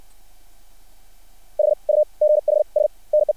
Morse code, heard as a keyed tone, about 60 wpm, spaced as for 43 wpm, 600 Hz. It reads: HUGHS NE
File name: archived / CRSnet